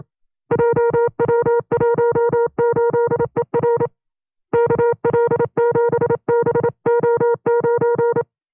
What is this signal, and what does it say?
Morse code, heard as a keyed tone, about 27 wpm, 470 Hz. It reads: JW18ER KL7BO9
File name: FreeFiles / CW-practice / 25WPM-side1